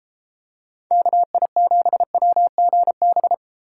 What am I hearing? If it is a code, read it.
Morse code, heard as a keyed tone, about 31 wpm, 700 Hz. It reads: KI7WGB